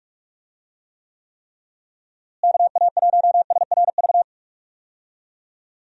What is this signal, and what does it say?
Morse code, heard as a keyed tone, about 46 wpm, 700 Hz. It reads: KA1SRV
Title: USA Call Signs - 6 Characters - Repeated 3x - 45wpm